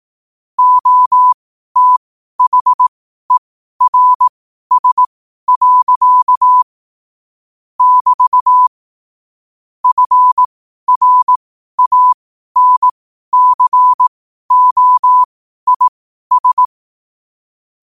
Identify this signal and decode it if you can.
Morse code, heard as a keyed tone, about 18 wpm, spaced as for 12 wpm, 1000 Hz. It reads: OTHERS. <BT> FRANCOIS
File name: qotdfiles / 12wpm